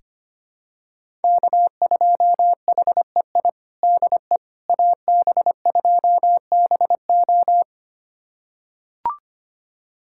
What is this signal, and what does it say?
Morse code, heard as a keyed tone, about 25 wpm, 700 Hz. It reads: K2HEI DE AB2BO
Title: USA Callsign DE Callsign - 5 Characters - 25wpm